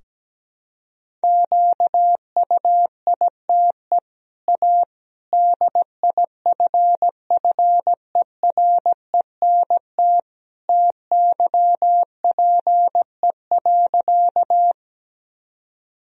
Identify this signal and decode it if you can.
Morse code, heard as a keyed tone, about 17 wpm, 700 Hz. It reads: QUITE A DIFFERENT TYPE.
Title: Sentences from Top 1000 Words - Review Part 3 - 17wpm